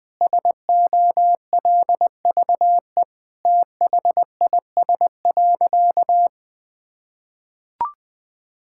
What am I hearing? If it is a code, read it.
Morse code, heard as a keyed tone, about 20 wpm, 700 Hz. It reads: SOLVE THIS.
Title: Sentences with 5-Word Phrases - 20wpm